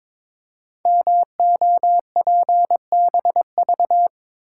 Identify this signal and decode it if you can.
Morse code, heard as a keyed tone, about 22 wpm, 700 Hz. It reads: MOPBV
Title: Code Groups - 5 Letters - 22wpm